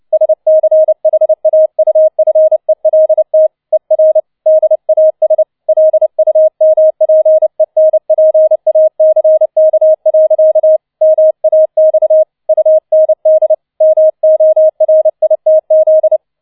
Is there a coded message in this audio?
Yes: Morse code from 0.1 to 16.2 s, heard as a keyed tone, about 29 wpm, 630 Hz. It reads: SCHAUFELT ER DAS LUMPENPACK. MAX UND MORITZ